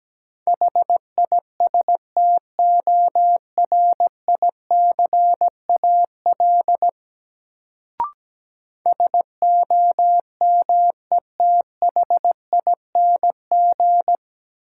Morse code, heard as a keyed tone, about 17 wpm, 700 Hz. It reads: HISTORICAL SOMETHING
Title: Sentences from Top 600 Words - New Word Drill - 17wpm